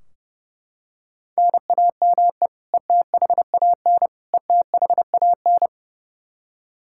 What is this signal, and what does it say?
Morse code, heard as a keyed tone, about 32 wpm, 700 Hz. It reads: NAME ETHAN ETHAN
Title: Practice QSO - Exchange 1 - Beginning - 30wpm